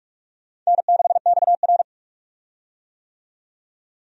Morse code, heard as a keyed tone, about 46 wpm, 700 Hz. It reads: N6XR
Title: USA Call Signs - 4 Characters - 45wpm